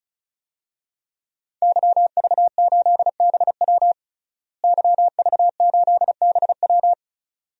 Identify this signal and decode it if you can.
Morse code, heard as a keyed tone, about 34 wpm, 700 Hz. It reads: YV8BW YV8BW
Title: DXCC-100 Call Signs - 5 Characters - Repeated 2x - 35wpm